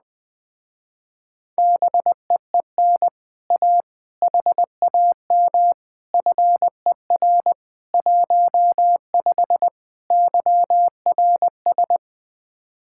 Morse code, heard as a keyed tone, about 20 wpm, 700 Hz. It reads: BEEN A HAM FER 15 YRS
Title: Speed-Racing Common QSO Phrases - 17wpm